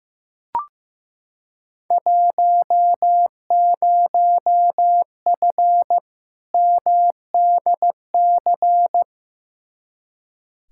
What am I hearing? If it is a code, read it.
Morse code, heard as a keyed tone, about 15 wpm, 700 Hz. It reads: 10F MDC
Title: Field Day Exchanges Encore - 15wpm